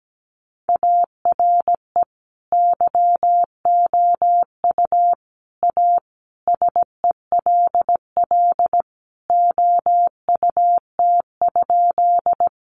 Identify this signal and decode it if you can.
Morse code, heard as a keyed tone, about 17 wpm, 700 Hz. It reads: ARE YOU A SELL OUT?
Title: Sentences from Top 1100 Words 17wpm